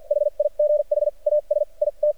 Morse code, heard as a keyed tone, about 53 wpm, spaced as for 41 wpm, 600 Hz. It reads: HIGH AS IT
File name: archived / CRSnet